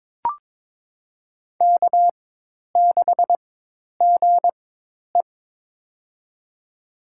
Morse code, heard as a keyed tone, about 22 wpm, spaced as for 10 wpm, 700 Hz. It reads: K6GE
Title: USA Call Signs - 4 Characters - 4x Character Spacing - 22wpm